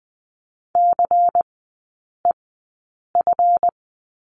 Morse code, heard as a keyed tone, about 20 wpm, spaced as for 8 wpm, 700 Hz. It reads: CEF
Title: 191 Lesson 31e - Letter v - Three Characters 20wpm